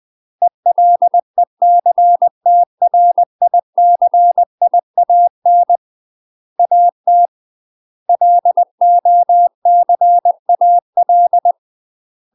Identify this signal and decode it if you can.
Morse code, heard as a keyed tone, about 20 wpm, 700 Hz. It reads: ELECTRICIAN AT LOCAL
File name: UK MASTER QSOs TRAINING / 20WPM / UK MASTER QSOs 0032